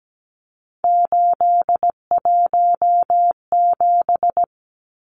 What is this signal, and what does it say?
Morse code, heard as a keyed tone, about 17 wpm, 700 Hz. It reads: 817
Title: Numbers - 3-Digit 17wpm